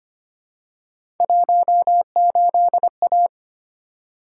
Morse code, heard as a keyed tone, about 25 wpm, 700 Hz. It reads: 18A
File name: Three Letter-Number Combinations - 25wpm